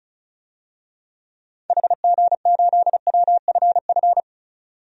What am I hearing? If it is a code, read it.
Morse code, heard as a keyed tone, about 34 wpm, 700 Hz. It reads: HG8WFF
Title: DXCC-100 Call Signs - 6 Characters - 35wpm